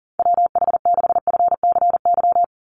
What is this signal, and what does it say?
Morse code, heard as a keyed tone, about 40 wpm, 700 Hz. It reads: WH6FCY
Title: USA Call Signs 40wpm